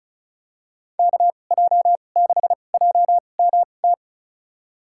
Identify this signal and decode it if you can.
Morse code, heard as a keyed tone, about 34 wpm, spaced as for 25 wpm, 700 Hz. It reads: KJ6JMT
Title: USA Call Signs - 6 Characters - 2x Character Spacing - 35wpm